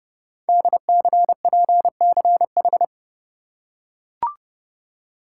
Morse code, heard as a keyed tone, about 30 wpm, 700 Hz. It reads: DCPCH
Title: Code Groups - 5 Letters - 30wpm